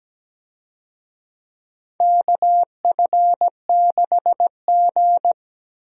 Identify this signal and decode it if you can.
Morse code, heard as a keyed tone, about 17 wpm, 700 Hz. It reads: KF6G
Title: USA Call Signs - 4 Characters - Repeated 3x - 17wpm